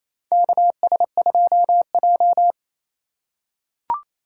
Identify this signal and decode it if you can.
Morse code, heard as a keyed tone, about 27 wpm, 700 Hz. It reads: KS2J
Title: USA Call Signs - 4 Characters - 4x Character Spacing - 28wpm